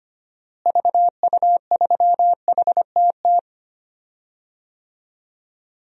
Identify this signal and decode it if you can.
Morse code, heard as a keyed tone, about 25 wpm, 700 Hz. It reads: VU3HTT
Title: DXCC-100 Call Signs - 6 Characters - Repeated 2x - 25wpm